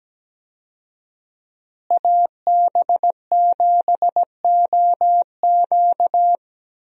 Morse code, heard as a keyed tone, about 17 wpm, 700 Hz. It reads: AB7OQ
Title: USA Call Signs - 5 Characters - Repeated 3x - 17wpm